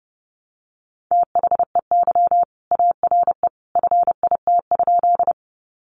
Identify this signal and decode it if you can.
Morse code, heard as a keyed tone, about 30 wpm, 700 Hz. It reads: THEY ARE FIT?
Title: Sentences from Top 800 Words 30wpm